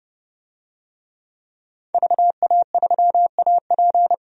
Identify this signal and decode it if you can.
Morse code, heard as a keyed tone, about 30 wpm, 700 Hz. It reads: VA3AP
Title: DXCC-100 Call Signs - 5 Characters - 30wpm